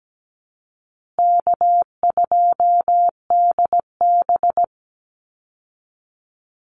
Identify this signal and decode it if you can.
Morse code, heard as a keyed tone, about 17 wpm, 700 Hz. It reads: K2DB
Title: Easy Call Signs Repeated 3x - 17wpm